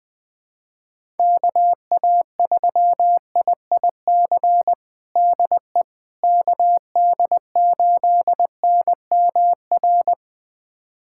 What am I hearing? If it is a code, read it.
Morse code, heard as a keyed tone, about 20 wpm, 700 Hz. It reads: KA3IIC DE KD8NMR
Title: USA Callsign DE Callsign - 6 Characters - 20wpm